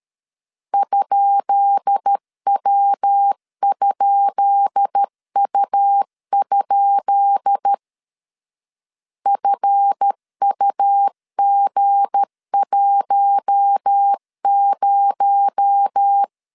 Morse code, heard as a keyed tone, about 13 wpm, 800 Hz. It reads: ?W?U? FUG10
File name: LEZIONI / 10 SIGNS / 10B INT 12wpm 2414